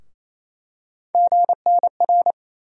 Morse code, heard as a keyed tone, about 28 wpm, 700 Hz. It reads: GNR
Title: USA Call Sign Postfixes - 3 Letters 28wpm